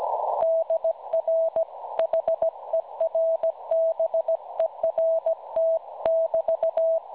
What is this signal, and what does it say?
Morse code, heard as a keyed tone, about 16 wpm, 650 Hz. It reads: DR HERBERT <BT>
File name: cwronde / websdr recording .6kHz